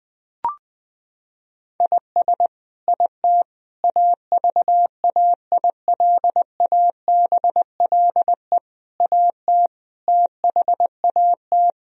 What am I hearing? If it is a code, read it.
Morse code, heard as a keyed tone, about 20 wpm, 700 Hz. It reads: IS IT AVAILABLE AT THAT
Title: Sentences from Top 1000 Words - Review Part 3 - 20wpm